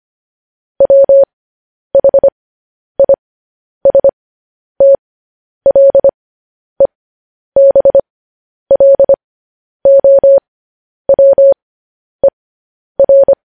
Morse code, heard as a keyed tone, about 25 wpm, spaced as for 10 wpm, 550 Hz. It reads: WHISTLEBLOWER
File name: podcast / News headlines-1x-25x10